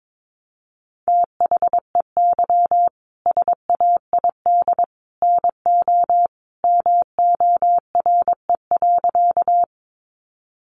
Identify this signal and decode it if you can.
Morse code, heard as a keyed tone, about 22 wpm, 700 Hz. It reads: THEY SAID NO MORE.
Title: Sentences from Top 100 Words 22wpm